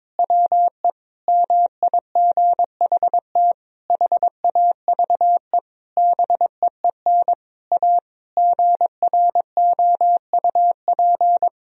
Morse code, heard as a keyed tone, about 21 wpm, 700 Hz. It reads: WE MIGHT HAVE BEEN A GROUP
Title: Sentences from Top 1000 Words - Review Part 1 - 22wpm